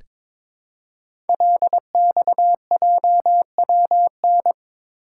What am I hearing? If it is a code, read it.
Morse code, heard as a keyed tone, about 22 wpm, 700 Hz. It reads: LXJWN
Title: Code Groups - 5 Letter:Numbers - 22wpm